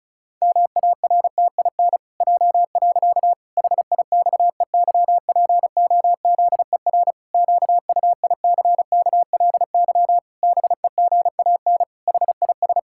Morse code, heard as a keyed tone, about 36 wpm, 700 Hz. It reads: MARTIN J. HIXEYPOZER QUICKLY BEGAN HIS
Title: Pangram Challenge v2 - 35wpm